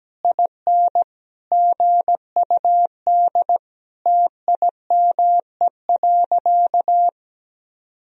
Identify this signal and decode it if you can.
Morse code, heard as a keyed tone, about 17 wpm, 700 Hz. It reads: IN GUD TIME.
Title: Sentences from Top 200 Words and Abbreviations 17wpm